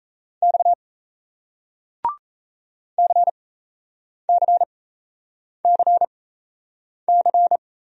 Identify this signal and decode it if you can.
Morse code, heard as a keyed tone, about 38 wpm, spaced as for 8 wpm, 700 Hz. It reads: X CCCC